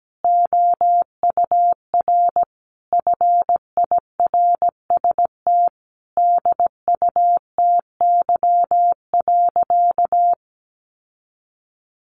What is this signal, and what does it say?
Morse code, heard as a keyed tone, about 17 wpm, 700 Hz. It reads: OUR FIRST DUTY.
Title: Sentences from Top 1500 Words 17wpm